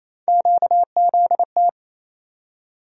Morse code, heard as a keyed tone, about 28 wpm, 700 Hz. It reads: QZT
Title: USA Call Sign Postfixes - 3 Letters 28wpm